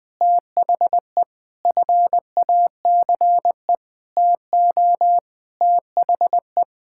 Morse code, heard as a keyed tone, about 20 wpm, 700 Hz. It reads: THE FACE TO THE
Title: Sentences with 8-Word Phrases - Spoken Beginning - 20wpm